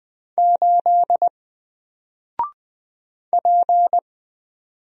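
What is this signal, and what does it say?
Morse code, heard as a keyed tone, about 20 wpm, 700 Hz. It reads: 8 P
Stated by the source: Single letter and number Morse drill at 20 wpm, from N4LCD.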